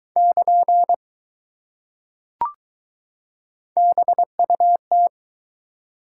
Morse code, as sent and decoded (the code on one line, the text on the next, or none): -.--. / -... ..- -
<KN> BUT